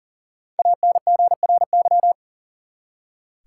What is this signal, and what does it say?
Morse code, heard as a keyed tone, about 40 wpm, 700 Hz. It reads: ANGRY
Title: Five Letter Words - 40wpm